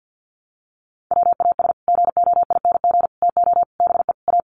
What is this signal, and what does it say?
Morse code, heard as a keyed tone, about 48 wpm, 700 Hz. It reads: WAS GOING TO BE A
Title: Sets of 5 Words 50wpm